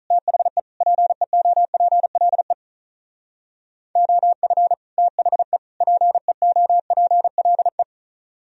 Morse code, heard as a keyed tone, about 46 wpm, 700 Hz. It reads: THE PEOPLE OF THE PEOPLE